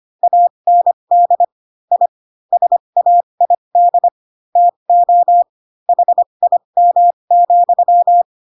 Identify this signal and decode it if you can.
Morse code, heard as a keyed tone, about 25 wpm, spaced as for 22 wpm, 700 Hz. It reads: AND I SAID TO HIM,
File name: Pinocchio / Pinocchio ch7 18 25 ik1hge